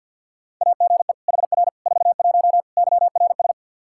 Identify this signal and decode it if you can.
Morse code, heard as a keyed tone, about 49 wpm, 700 Hz. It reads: AGE HR 41 YRS
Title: Practice QSO - Exchange 3 - More Info - 50wpm